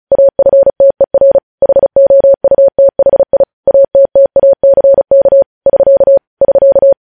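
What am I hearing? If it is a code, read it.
Morse code, heard as a keyed tone, about 36 wpm, 550 Hz. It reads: AFTER HOUTHI ATTACK <SK> <SK>
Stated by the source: News headlines sent in Morse at 35 WPM.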